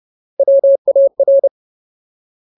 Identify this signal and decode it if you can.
Morse code, heard as a keyed tone, about 30 wpm, 550 Hz. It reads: WAR
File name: Ultimate Word List / 30WPM Ultimate